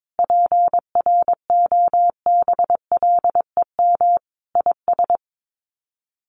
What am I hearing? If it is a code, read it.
Morse code, heard as a keyed tone, about 22 wpm, 700 Hz. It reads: PROBLEM IS
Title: Sets of 5 Words Encore2 22wpm